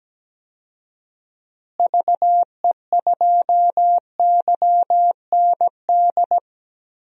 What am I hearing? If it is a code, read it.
Morse code, heard as a keyed tone, about 17 wpm, 700 Hz. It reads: VE2YND
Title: DXCC-100 Call Signs - 6 Characters - Repeated 3x - 17wpm